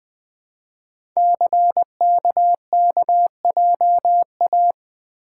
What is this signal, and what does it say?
Morse code, heard as a keyed tone, about 20 wpm, 700 Hz. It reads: CKKJA
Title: Code Groups - 5 Letters - 20wpm